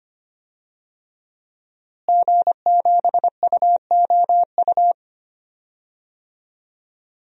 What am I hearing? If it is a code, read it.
Morse code, heard as a keyed tone, about 25 wpm, 700 Hz. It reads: G7UOU
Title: DXCC-100 Call Signs - 5 Characters - Repeated 3x - 25wpm